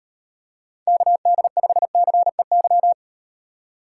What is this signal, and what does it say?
Morse code, heard as a keyed tone, about 38 wpm, 700 Hz. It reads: KD5CEY